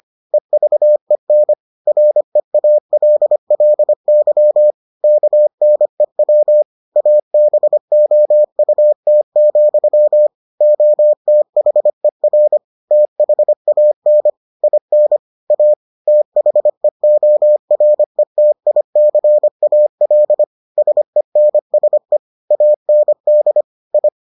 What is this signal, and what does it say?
Morse code, heard as a keyed tone, about 25 wpm, 600 Hz. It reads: EVEN REALLY KNEW ABOUT, OTHER THAN IN A THEORETICAL SENSE AND I